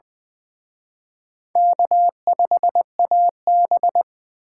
Morse code, heard as a keyed tone, about 24 wpm, 700 Hz. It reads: K5AB
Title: Speed Racing - USA 1x2 Call Signs 20wpm